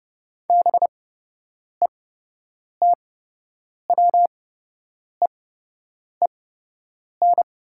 Morse code, heard as a keyed tone, about 30 wpm, spaced as for 8 wpm, 700 Hz. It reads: BETWEEN
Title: 8x Character Spacing - Top 500 Words - 30wpm